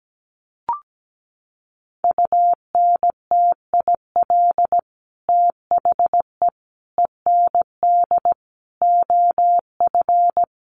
Morse code, heard as a keyed tone, about 17 wpm, 700 Hz. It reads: UNTIL THE END OF
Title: Sets of 5 Words Encore 17wpm